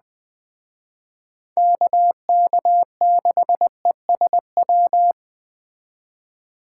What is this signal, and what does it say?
Morse code, heard as a keyed tone, about 20 wpm, 700 Hz. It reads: KK6ESW
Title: Speed Racing - USA 2x3 Call Signs 17wpm